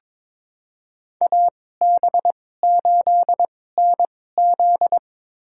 Morse code, heard as a keyed tone, about 22 wpm, spaced as for 16 wpm, 700 Hz. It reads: AB8NZ
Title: USA Call Signs - 5 Characters - 2x Character Spacing - 22wpm